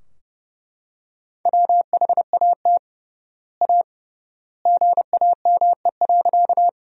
Morse code, heard as a keyed tone, about 32 wpm, 700 Hz. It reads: WHAT A GAME.
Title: 3x Word-Spacing - Sentences from Top 500 Words - 30wpm